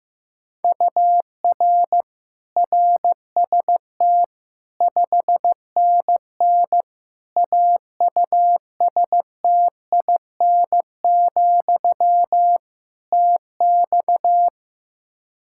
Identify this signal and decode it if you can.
Morse code, heard as a keyed tone, about 15 wpm, 700 Hz. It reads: UR RST 5NN AUSTIN, TX
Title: Practice QSO - Exchange 1 - Beginning - 15wpm